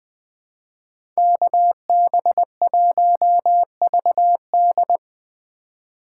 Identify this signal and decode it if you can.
Morse code, heard as a keyed tone, about 22 wpm, 700 Hz. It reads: KB1VD
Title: Speed Racing - USA 2x2 Call Signs 20wpm